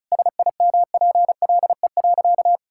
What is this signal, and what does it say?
Morse code, heard as a keyed tone, about 34 wpm, 700 Hz. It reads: SIMPLE.
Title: Sentences from Top 1000 Words - Review Part 6 - 35wpm